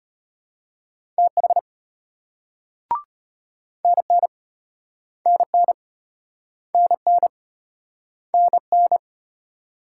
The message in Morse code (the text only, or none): TH NN NN NN NN